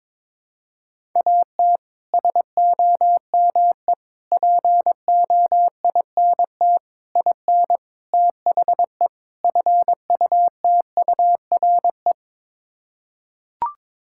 Morse code, heard as a keyed tone, about 22 wpm, 700 Hz. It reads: AT SOME POINT IN THE FUTURE
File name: practice / 22wpm / Sets of 6 Words - 22wpm